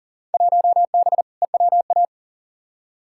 Morse code, heard as a keyed tone, about 38 wpm, 700 Hz. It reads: 1B EWA
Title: Field Day Exchanges Encore - 40wpm